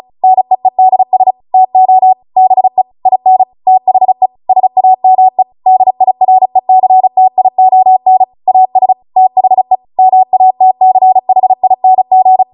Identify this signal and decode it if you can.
Morse code, heard as a keyed tone, about 36 wpm, 750 Hz. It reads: NEEDS TO BE IN THE SAME DIRECTION AS THE MATCHING